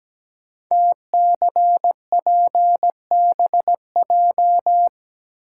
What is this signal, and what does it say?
Morse code, heard as a keyed tone, about 17 wpm, 700 Hz. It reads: TCPBJ
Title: Code Groups - 5 Letters - 17wpm